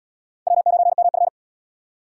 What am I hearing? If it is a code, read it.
Morse code, heard as a keyed tone, about 73 wpm, 700 Hz. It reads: W9MF